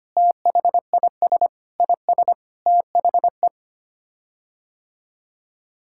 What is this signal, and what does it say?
Morse code, heard as a keyed tone, about 25 wpm, 700 Hz. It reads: THIS IS THE
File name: Sets of 9 Words - 25wpm